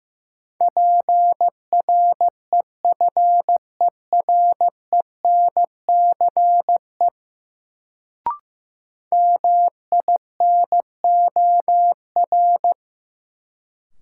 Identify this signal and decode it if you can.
Morse code, heard as a keyed tone, about 15 wpm, 700 Hz. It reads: PREFERENCE MINOR
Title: Sentences from Top 1500 Words - New Word Drill Alt - 15wpm